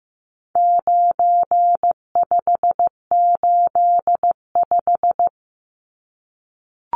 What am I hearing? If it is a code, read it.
Morse code, heard as a keyed tone, about 15 wpm, 700 Hz. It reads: 9585
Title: Numbers - 4-Digit 15wpm